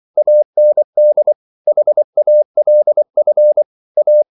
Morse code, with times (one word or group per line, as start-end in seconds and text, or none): 0.2-1.3 AND
1.7-3.6 HALF
4.0-4.2 A